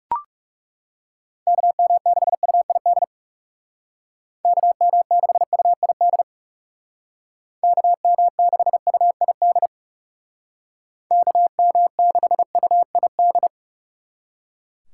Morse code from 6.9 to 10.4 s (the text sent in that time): KM6UID